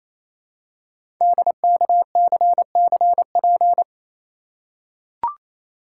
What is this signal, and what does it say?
Morse code, heard as a keyed tone, about 27 wpm, 700 Hz. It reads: DKCCP
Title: Code Groups - 5 Letters - 28wpm